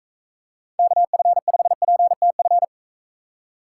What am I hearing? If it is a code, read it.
Morse code, heard as a keyed tone, about 42 wpm, 700 Hz. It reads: KF5PTF